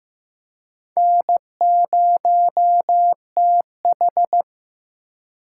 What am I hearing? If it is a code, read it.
Morse code, heard as a keyed tone, about 18 wpm, 700 Hz. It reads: N0TH
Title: Speed Racing - USA 1x2 Call Signs 15wpm